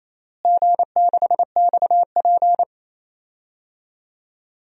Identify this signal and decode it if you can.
Morse code, heard as a keyed tone, about 28 wpm, 700 Hz. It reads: G6XP
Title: DXCC-100 Call Signs - 4 Characters - Repeated 3x - 28wpm